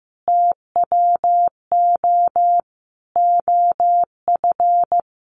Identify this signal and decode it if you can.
Morse code, heard as a keyed tone, about 15 wpm, 700 Hz. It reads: TWO OF
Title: Sentences from Top 100 Words 15wpm